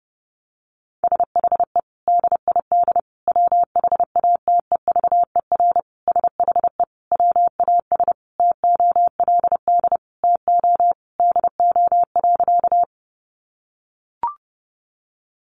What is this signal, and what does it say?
Morse code, heard as a keyed tone, about 28 wpm, 700 Hz. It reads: SHE DID WHATEVER SHE WAS TOLD TO DO.
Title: Sentences from Top 1100 Words 30wpm